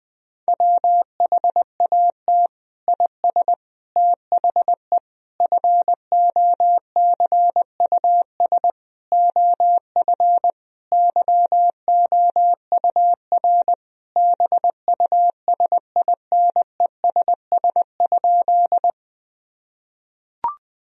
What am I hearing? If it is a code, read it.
Morse code, heard as a keyed tone, about 20 wpm, 700 Hz. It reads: WHAT IS THE FOCUS OF YOUR BUSINESS?
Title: Sentences from Top 1000 Words - Review Part 4 - 20wpm